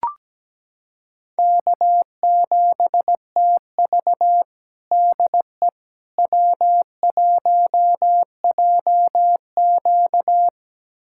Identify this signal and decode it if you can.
Morse code, heard as a keyed tone, about 17 wpm, 700 Hz. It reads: K7TV DE W1JQ
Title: USA Callsign DE Callsign - 4 Characters - 17wpm